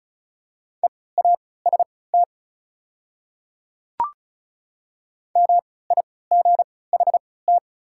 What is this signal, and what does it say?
Morse code, heard as a keyed tone, about 34 wpm, spaced as for 20 wpm, 690 Hz. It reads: EAST MIGHT
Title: 3x Character Spacing - Top 500 Words - 35wpm